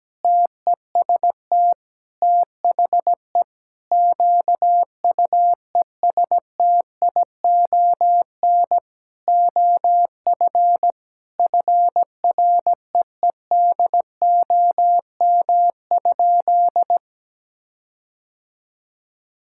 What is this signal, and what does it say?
Morse code, heard as a keyed tone, about 17 wpm, 700 Hz. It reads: TEST THE QUESTION OF FREEDOM?
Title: Sentences from Top 1600 Words 17wpm